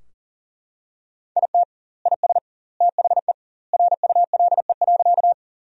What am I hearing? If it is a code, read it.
Morse code, heard as a keyed tone, about 40 wpm, 700 Hz. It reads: IT IS THE RULE.
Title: 2x Word-Spacing - Sentences from Top 400 Words - 40wpm